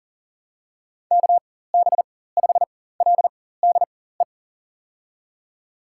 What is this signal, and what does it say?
Morse code, heard as a keyed tone, about 40 wpm, spaced as for 19 wpm, 700 Hz. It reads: KB5LDE
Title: USA Call Signs - 4x Character Spacing - 40wpm